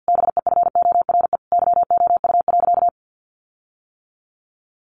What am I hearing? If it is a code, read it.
Morse code, heard as a keyed tone, about 50 wpm, 700 Hz. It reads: BEFORE YOU.
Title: Sentences from Top 1400 Words 50wpm